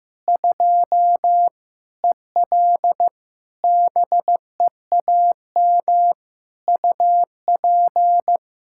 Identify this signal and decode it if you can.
Morse code, heard as a keyed tone, about 15 wpm, 700 Hz. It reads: 2 EL BEAM UP
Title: Practice QSO - Exchange 2 - Info - 15wpm